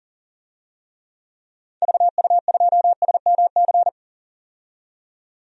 Morse code, heard as a keyed tone, about 40 wpm, 700 Hz. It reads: VU2SMC
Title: DXCC-100 Call Signs - 6 Characters - 40wpm